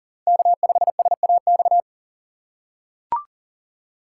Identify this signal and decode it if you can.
Morse code, heard as a keyed tone, about 40 wpm, 700 Hz. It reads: K5SAX
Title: USA Call Signs - 3x Character Spacing - 40wpm